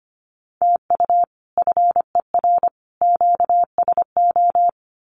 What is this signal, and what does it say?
Morse code, heard as a keyed tone, about 25 wpm, 700 Hz. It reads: TU FER QSO
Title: Common QSO Phrases 25wpm